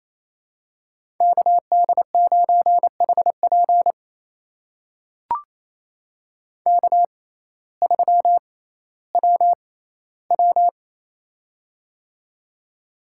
Morse code, heard as a keyed tone, about 28 wpm, spaced as for 15 wpm, 700 Hz. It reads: KD9HP K3WW
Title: USA Call Signs - 6x Character Spacing - 28wpm